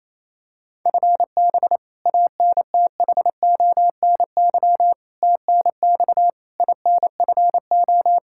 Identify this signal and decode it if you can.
Morse code, heard as a keyed tone, about 28 wpm, 700 Hz. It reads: FB ANTHONY TNX INFO
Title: K1USN SST QSO Practice 28wpm